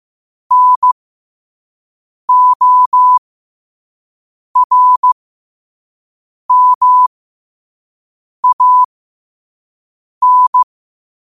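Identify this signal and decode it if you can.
Morse code, heard as a keyed tone, about 15 wpm, spaced as for 5 wpm, 1000 Hz. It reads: NORMAN
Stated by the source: Morse code quote of the day at 5 wpm.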